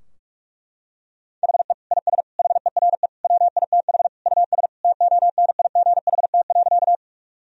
Morse code, heard as a keyed tone, about 46 wpm, 700 Hz. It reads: HE IS HERE WITH US TONIGHT.
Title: Sentences from Top 1000 Words - Review Part 3 - 45wpm